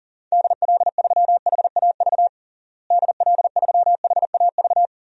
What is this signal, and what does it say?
Morse code, heard as a keyed tone, about 40 wpm, 700 Hz. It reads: DL3HAV DL3HAV
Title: DXCC-100 Call Signs - 6 Characters - Repeated 2x - 40wpm